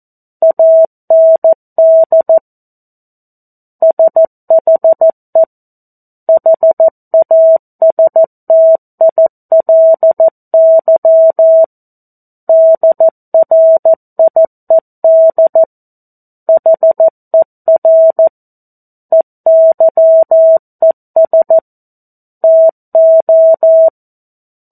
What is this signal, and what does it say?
Morse code, heard as a keyed tone, about 14 wpm, 650 Hz. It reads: AND SHE HASTILY DRIED HER EYES TO
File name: 02-Chapter-02 14 WPM